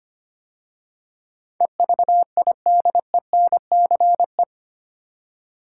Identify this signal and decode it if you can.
Morse code, heard as a keyed tone, about 25 wpm, 690 Hz. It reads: EVIDENCE
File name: 8 Letter Words - First 3 Letters Given - 25wpm